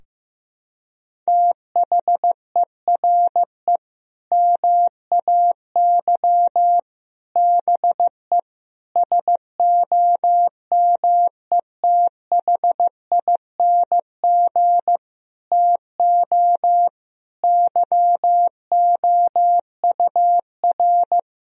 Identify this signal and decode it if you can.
Morse code, heard as a keyed tone, about 15 wpm, 700 Hz. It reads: THERE MAY BE SOMETHING TO YOUR
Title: Sentences from Top 1000 Words - Review Part 3 - 15wpm